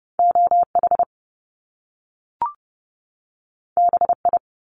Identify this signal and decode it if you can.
Morse code, heard as a keyed tone, about 30 wpm, 700 Hz. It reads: OH BI
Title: Top Two Letter Combinations in Words 30wpm